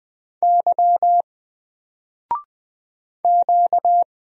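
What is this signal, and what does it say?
Morse code, heard as a keyed tone, about 20 wpm, 700 Hz. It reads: Y Q
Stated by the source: Single letter and number Morse drill at 20 wpm, from N4LCD.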